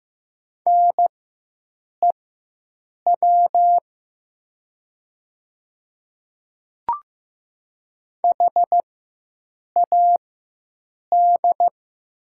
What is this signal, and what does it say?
Morse code, heard as a keyed tone, about 15 wpm, spaced as for 7 wpm, 700 Hz. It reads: NEW HAD
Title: 4x Character Spacing - Top 200 Words - 15wpm